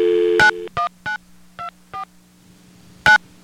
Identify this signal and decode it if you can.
DTMF, heard as telephone touch tones, tones of about 120 ms. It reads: 91924 9